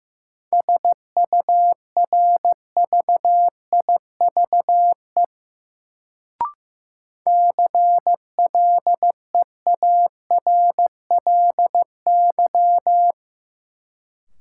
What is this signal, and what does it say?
Morse code, heard as a keyed tone, about 15 wpm, 700 Hz. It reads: SURVIVE CLEARLY
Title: Sentences from Top 1300 Words - New Word Drill - 15wpm